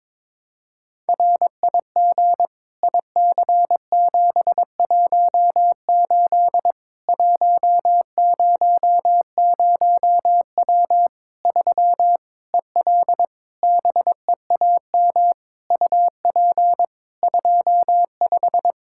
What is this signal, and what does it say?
Morse code, heard as a keyed tone, about 22 wpm, 700 Hz. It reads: RIG IC718 100W 3 EL BEAM UP 25
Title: Practice QSO - Exchange 2 - Info - 22wpm